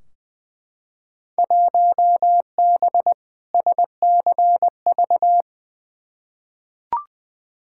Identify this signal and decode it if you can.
Morse code, heard as a keyed tone, about 20 wpm, 700 Hz. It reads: 1B SCV
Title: Field Day Exchanges Encore - 20wpm